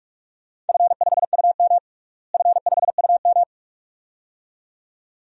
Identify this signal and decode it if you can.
Morse code, heard as a keyed tone, about 42 wpm, 700 Hz. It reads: F5UM F5UM
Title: DXCC-100 Call Signs - 4 Characters - Repeated 2x - 45wpm